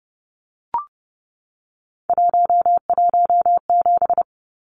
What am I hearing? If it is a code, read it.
Morse code, heard as a keyed tone, about 29 wpm, 700 Hz. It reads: 117
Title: Numbers - 3-Digit 30wpm